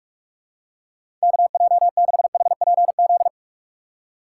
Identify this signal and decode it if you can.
Morse code, heard as a keyed tone, about 48 wpm, 700 Hz. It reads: KJ6HPZ
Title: USA Call Signs - 6 Characters - 6x Character Spacing - 45wpm